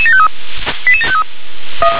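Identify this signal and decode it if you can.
Five-tone selective call (ZVEI), heard as quick runs of five tones, tones of about 70 ms. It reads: 88753
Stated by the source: ZVEI five-tone selective call, code 88727, received by SDR.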